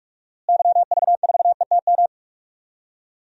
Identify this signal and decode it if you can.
Morse code, heard as a keyed tone, about 41 wpm, 700 Hz. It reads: YV4ETM
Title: DXCC-100 Call Signs - 6 Characters - Repeated 3x - 45wpm